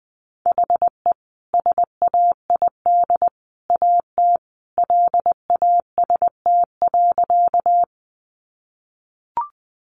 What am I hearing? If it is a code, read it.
Morse code, heard as a keyed tone, about 20 wpm, 700 Hz. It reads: HE SAID AT LAST.
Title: Sentences from Top 300 Words 20wpm